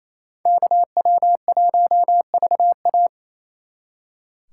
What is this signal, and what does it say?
Morse code, heard as a keyed tone, about 35 wpm, 700 Hz. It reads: KW1VA